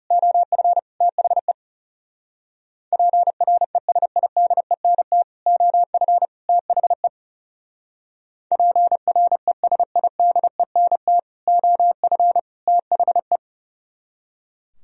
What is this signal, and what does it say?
Morse code, heard as a keyed tone, about 37 wpm, 700 Hz. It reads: OF THE PRESIDENT OF THE PRESIDENT OF THE
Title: Speed-Racing - Sets of 3 Words 30wpm